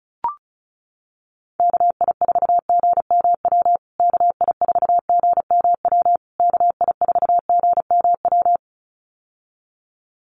Morse code, heard as a keyed tone, about 34 wpm, 700 Hz. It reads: KI4GMW KI4GMW KI4GMW
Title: USA Call Signs Repeated 3x 35wpm